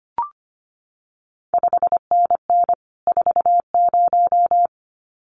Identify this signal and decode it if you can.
Morse code, heard as a keyed tone, about 25 wpm, 700 Hz. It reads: 5NN 40
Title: CQ WW DX Contest Exchange Encore - 25wpm